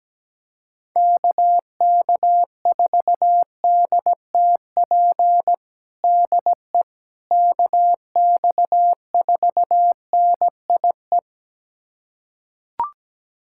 Morse code, heard as a keyed tone, about 17 wpm, 700 Hz. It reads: KK4DTP DE KX4NIE
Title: USA Callsign DE Callsign - 6 Characters - 17wpm